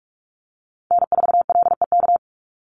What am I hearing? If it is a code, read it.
Morse code, heard as a keyed tone, about 46 wpm, 700 Hz. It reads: N4LEK
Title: USA Call Signs 45wpm